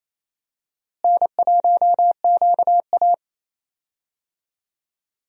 Morse code, heard as a keyed tone, about 28 wpm, 700 Hz. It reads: N1QA
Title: USA Call Signs - 4 Characters - Repeated 2x - 28wpm